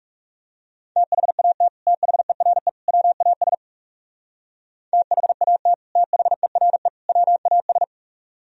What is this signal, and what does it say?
Morse code, heard as a keyed tone, about 39 wpm, 700 Hz. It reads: THAT THERE WAS THAT THERE WAS